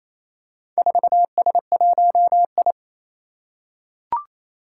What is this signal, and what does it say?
Morse code, heard as a keyed tone, about 29 wpm, 700 Hz. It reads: 4S1I
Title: Four Challenging Letter-Number Combinations 28wpm